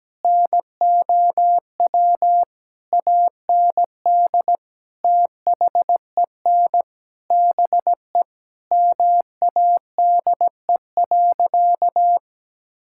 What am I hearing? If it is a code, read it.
Morse code, heard as a keyed tone, about 17 wpm, 700 Hz. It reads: NOW AND THEN BE MADE.
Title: Sentences from Top 1600 Words 17wpm